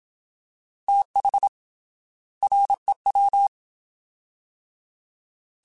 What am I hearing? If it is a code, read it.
Morse code, heard as a keyed tone, about 27 wpm, 800 Hz. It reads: TH REW
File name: TH / TH mp3s / Fast TH @25